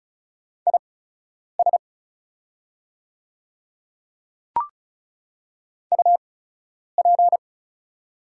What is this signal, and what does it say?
Morse code, heard as a keyed tone, about 36 wpm, spaced as for 10 wpm, 700 Hz. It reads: IS UP
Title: 8x Character Spacing - Top 100 Words - 35wpm